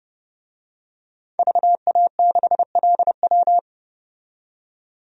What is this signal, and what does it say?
Morse code, heard as a keyed tone, about 30 wpm, 700 Hz. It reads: VA6LW
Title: DXCC-100 Call Signs - 5 Characters - 30wpm